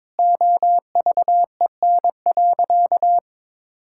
Morse code, heard as a keyed tone, about 22 wpm, 700 Hz. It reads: OVEN.
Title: Sentences from Top 1000 Words - Review Part 6 - 22wpm